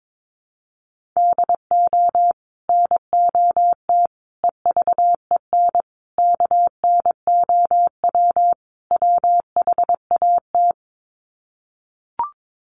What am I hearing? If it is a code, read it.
Morse code, heard as a keyed tone, about 22 wpm, 700 Hz. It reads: DO NOT EVEN KNOW WHAT
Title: Sets of 5 Words Encore 22wpm